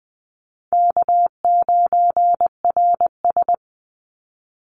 Morse code, heard as a keyed tone, about 20 wpm, 700 Hz. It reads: K9RS
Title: Call Signs - 20wpm